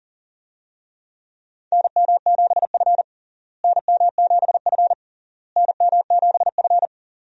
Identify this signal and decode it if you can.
Morse code, heard as a keyed tone, about 40 wpm, 700 Hz. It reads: NM7F NM7F NM7F
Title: USA Call Signs - 4 Characters - Repeated 3x - 40wpm